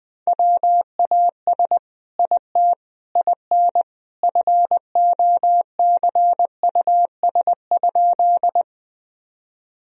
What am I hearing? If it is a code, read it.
Morse code, heard as a keyed tone, about 20 wpm, 700 Hz. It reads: WAS IT IN FOCUS?
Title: Sentences from Top 1000 Words - Review Part 4 - 20wpm